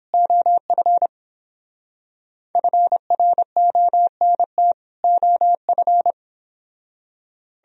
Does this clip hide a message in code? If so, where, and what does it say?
Yes: Morse code from 0.1 to 6.1 s, heard as a keyed tone, about 29 wpm, 700 Hz. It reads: OF FRONT OF